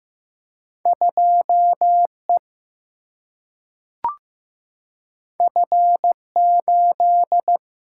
Morse code, heard as a keyed tone, about 15 wpm, 700 Hz. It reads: 2 E F 8
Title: Single-Letter Single-Number 15wpm